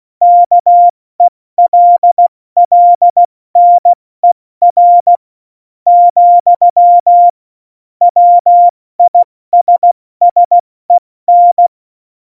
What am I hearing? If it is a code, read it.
Morse code, heard as a keyed tone, about 16 wpm, spaced as for 13 wpm, 700 Hz. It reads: KELLNER, WISSEN
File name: Morsekurs / Witze / Witz 23. April 2019, 70 F80